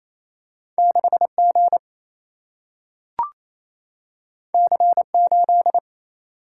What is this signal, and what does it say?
Morse code, heard as a keyed tone, about 28 wpm, 700 Hz. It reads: 6G C8
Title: Single-Letter Single-Number 28wpm